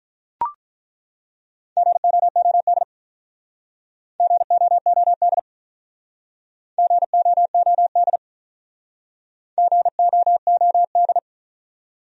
Morse code, heard as a keyed tone, about 46 wpm, 700 Hz. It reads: GOOD GOOD GOOD GOOD